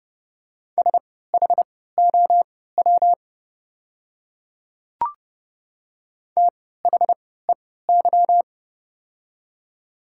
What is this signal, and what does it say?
Morse code, heard as a keyed tone, about 29 wpm, spaced as for 17 wpm, 700 Hz. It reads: SHOW THEY
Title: 3x Character Spacing - Top 200 Words - 30wpm